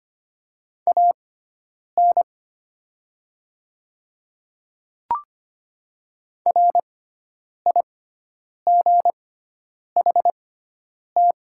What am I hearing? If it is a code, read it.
Morse code, heard as a keyed tone, about 25 wpm, spaced as for 9 wpm, 700 Hz. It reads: AN RIGHT
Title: 6x Character Spacing - Top 300 Words - 25wpm